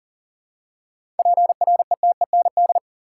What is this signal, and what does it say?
Morse code, heard as a keyed tone, about 37 wpm, 700 Hz. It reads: PRETEND
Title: Seven Letter Words - 40wpm